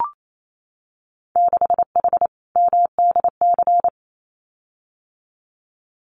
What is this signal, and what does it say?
Morse code, heard as a keyed tone, about 28 wpm, 700 Hz. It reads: 6H MDC
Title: Winter Field Day Exchanges Encore 28wpm